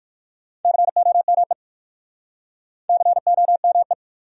Morse code, heard as a keyed tone, about 55 wpm, 700 Hz. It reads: COME COME
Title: Speed-Racing Top 500 Words 45wpm